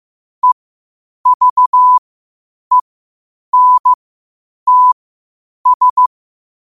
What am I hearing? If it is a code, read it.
Morse code, heard as a keyed tone, about 15 wpm, spaced as for 8 wpm, 1000 Hz. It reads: EVENTS
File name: qotdfiles / 8wpm